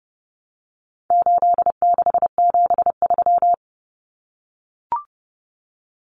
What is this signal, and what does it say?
Morse code, heard as a keyed tone, about 30 wpm, 700 Hz. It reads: 8673
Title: Common Cut Numbers - 4-Digit 30wpm